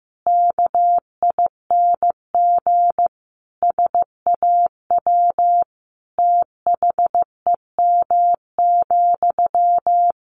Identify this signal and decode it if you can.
Morse code, heard as a keyed tone, about 15 wpm, 700 Hz. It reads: KING SAW THEM,
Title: Sentences from Top 1100 Words 15wpm